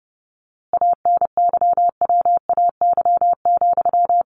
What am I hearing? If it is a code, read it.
Morse code, heard as a keyed tone, about 30 wpm, 700 Hz. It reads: ANYWAY,
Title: A Sound Of Thunder 30wpm